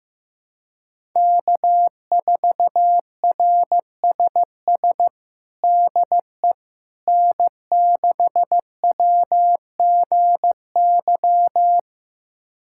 Morse code, heard as a keyed tone, about 15 wpm, 700 Hz. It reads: K4RSS DE N6WGY
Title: USA Callsign DE Callsign 15wpm